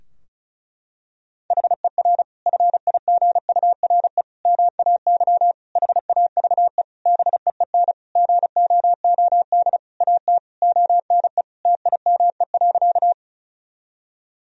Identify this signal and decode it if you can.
Morse code, heard as a keyed tone, about 34 wpm, 700 Hz. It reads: HER FIGURE MAY HAVE BEEN GOOD AT ONE TIME.
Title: Sentences from Top 1000 Words - Review Part 2 - 35wpm